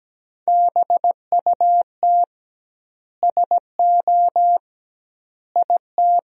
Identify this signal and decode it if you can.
Morse code, heard as a keyed tone, about 17 wpm, 700 Hz. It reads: BUT SO IT
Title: 2x Word-Spacing - Sentences from Top 100 Words - 17wpm